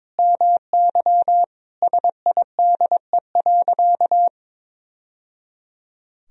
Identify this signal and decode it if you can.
Morse code, heard as a keyed tone, about 22 wpm, 700 Hz. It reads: MY SIDE.
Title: Sentences from Top 1000 Words - Review Part 3 - 22wpm